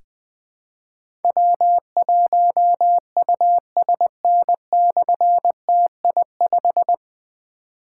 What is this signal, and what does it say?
Morse code, heard as a keyed tone, about 20 wpm, 700 Hz. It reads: W1USN/TI5
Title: Difficult Call Signs 20wpm